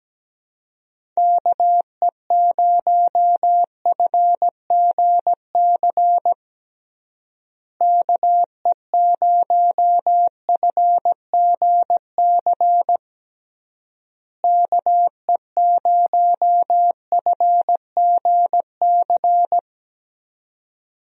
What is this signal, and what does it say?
Morse code, heard as a keyed tone, about 17 wpm, 700 Hz. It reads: KE0FGC KE0FGC KE0FGC
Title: USA Call Signs - 6 Characters - Repeated 3x - 17wpm